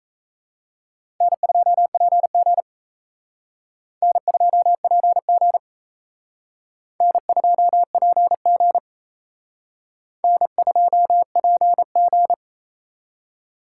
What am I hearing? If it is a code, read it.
Morse code, heard as a keyed tone, about 36 wpm, 700 Hz. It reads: N2PG N2PG N2PG N2PG